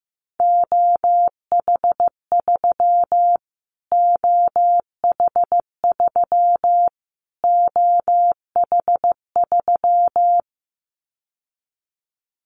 Morse code, heard as a keyed tone, about 14 wpm, 700 Hz. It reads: OH3 OH3 OH3
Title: Callsign Prefixes for Top 25 DXCC - Repeated 3x - 15wpm